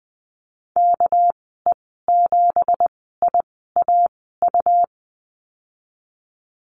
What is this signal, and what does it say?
Morse code, heard as a keyed tone, about 20 wpm, spaced as for 15 wpm, 700 Hz. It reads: KE7IAU
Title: 207 Lesson 33g - Letter j - Call Signs 20wpm